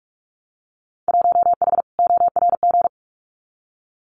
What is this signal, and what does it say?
Morse code, heard as a keyed tone, about 44 wpm, 700 Hz. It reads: 1H ORG
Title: Winter Field Day Exchanges 45wpm